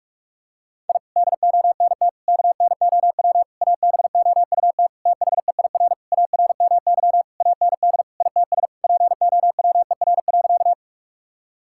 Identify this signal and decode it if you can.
Morse code, heard as a keyed tone, about 47 wpm, 700 Hz. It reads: I DONT KNOW ABOUT THEIR ARMY AND ITS POWER.
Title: Sentences from Top 1000 Words - Review Part 5 - 45wpm